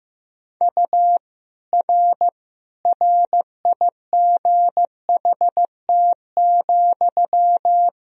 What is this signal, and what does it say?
Morse code, heard as a keyed tone, about 15 wpm, 700 Hz. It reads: U R RIGHT,
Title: Sentences from Top 400 Words and Abbreviations 15wpm